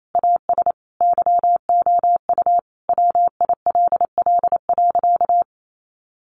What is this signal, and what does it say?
Morse code, heard as a keyed tone, about 28 wpm, 700 Hz. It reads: AS YOU WILL.
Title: Sentences from Top 100 Words 28wpm